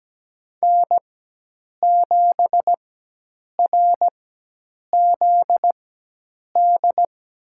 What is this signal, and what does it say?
Morse code, heard as a keyed tone, about 17 wpm, spaced as for 8 wpm, 700 Hz. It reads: N7RZD
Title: USA Call Signs - 5 Characters - 4x Character Spacing - 17wpm